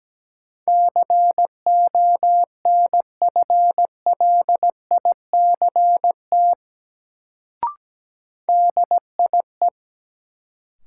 Morse code, heard as a keyed tone, about 17 wpm, 700 Hz. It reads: CONFLICT DIE
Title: Sentences from Top 1300 Words - New Word Drill - 17wpm